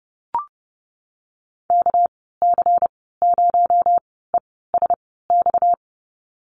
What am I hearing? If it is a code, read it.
Morse code, heard as a keyed tone, about 30 wpm, spaced as for 17 wpm, 700 Hz. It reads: KC0ESX
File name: Lesson 37g - Letter x - Call Signs 30wpm